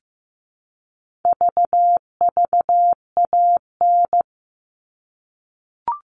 Morse code, heard as a keyed tone, about 15 wpm, 700 Hz. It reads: VVAN
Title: All Cut Numbers - 4-Digit 15wpm